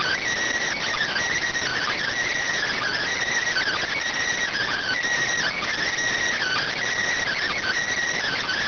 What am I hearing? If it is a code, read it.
Emergency Alert System (SAME) tones (header bursts), heard as harsh buzzing bursts. It is heard from 0.7 to 7.7 s.